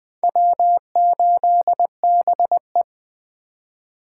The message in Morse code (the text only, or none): W8BE